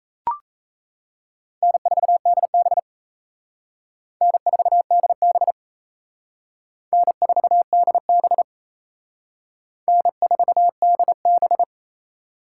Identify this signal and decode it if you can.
Morse code, heard as a keyed tone, about 34 wpm, 700 Hz. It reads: N4DB N4DB N4DB N4DB
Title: Speed Racing - USA 1x2 Call Signs 28wpm